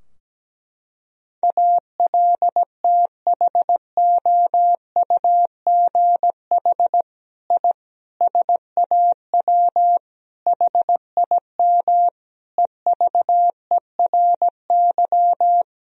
Morse code, heard as a keyed tone, about 17 wpm, 700 Hz. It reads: ALTHOUGH I SAW HIM EVERY
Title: Sentences from Top 1000 Words - Review Part 3 - 17wpm